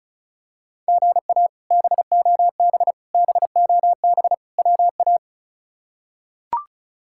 Morse code, heard as a keyed tone, about 36 wpm, 700 Hz. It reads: GA BOB BOB WA
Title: K1USN SST QSO Practice - 35wpm